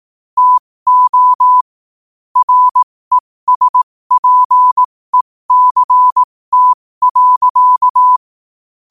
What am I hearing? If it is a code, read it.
Morse code, heard as a keyed tone, about 18 wpm, spaced as for 15 wpm, 1000 Hz. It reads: TO RESPECT.
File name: qotdfiles / 15wpm